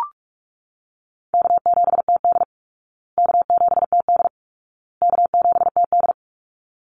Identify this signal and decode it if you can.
Morse code, heard as a keyed tone, about 44 wpm, 700 Hz. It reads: K7TD K7TD K7TD
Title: Easy Call Signs Repeated 3x - 45wpm